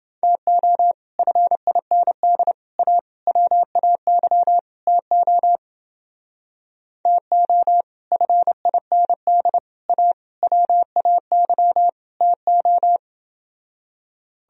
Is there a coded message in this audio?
Yes: Morse code from 0.2 to 13.0 s, heard as a keyed tone, about 28 wpm, 700 Hz. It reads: TO FIND A WAY TO TO FIND A WAY TO